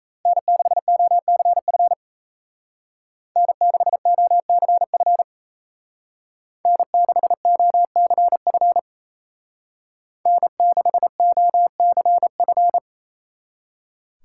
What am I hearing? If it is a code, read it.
Morse code, heard as a keyed tone, about 34 wpm, 700 Hz. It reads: N6OCF N6OCF N6OCF N6OCF
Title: Speed Racing - USA 1x3 Call Signs 28wpm